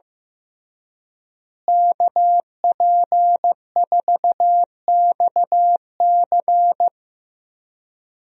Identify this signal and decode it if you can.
Morse code, heard as a keyed tone, about 16 wpm, 700 Hz. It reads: KP4XC
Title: DXCC-100 Call Signs - 5 Characters - Repeated 3x - 15wpm